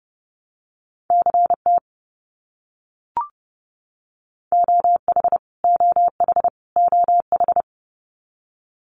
Morse code, heard as a keyed tone, about 29 wpm, 700 Hz. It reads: CT OH OH OH
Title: US States Repeated 3x 30wpm